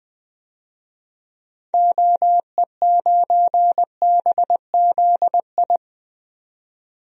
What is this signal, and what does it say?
Morse code, heard as a keyed tone, about 20 wpm, 700 Hz. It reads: OE9BZI
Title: DXCC-100 Call Signs - 6 Characters - 20wpm